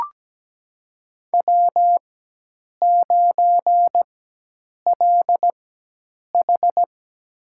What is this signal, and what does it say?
Morse code, heard as a keyed tone, about 17 wpm, spaced as for 8 wpm, 700 Hz. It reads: W9LH
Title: USA Call Signs - 4 Characters - 4x Character Spacing - 17wpm